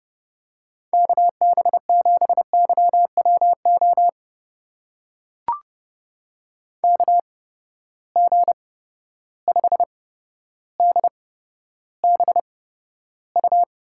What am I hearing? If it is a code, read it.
Morse code, heard as a keyed tone, about 30 wpm, spaced as for 13 wpm, 700 Hz. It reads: KB7YWO KG5DBU
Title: USA Call Signs - 6 Characters - 8x Character Spacing - 30wpm